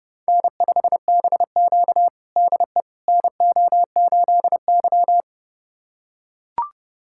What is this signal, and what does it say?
Morse code, heard as a keyed tone, about 30 wpm, 700 Hz. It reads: N5BQ DE NO8Y
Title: USA Callsign DE Callsign - 4 Characters - 30wpm